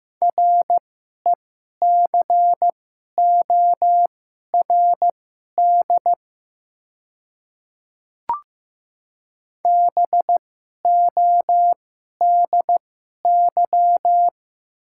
Morse code, heard as a keyed tone, about 15 wpm, spaced as for 11 wpm, 700 Hz. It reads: RECORD BODY
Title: 2x Character Spacing - Top 500 Words - 15wpm